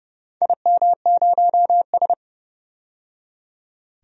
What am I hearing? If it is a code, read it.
Morse code, heard as a keyed tone, about 31 wpm, 700 Hz. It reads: IM0S
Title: DXCC-100 Call Signs - 4 Characters - 30wpm